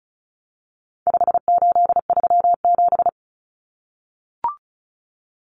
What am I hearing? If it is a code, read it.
Morse code, heard as a keyed tone, about 34 wpm, 700 Hz. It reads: 5837
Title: Numbers - 4-Digit 35wpm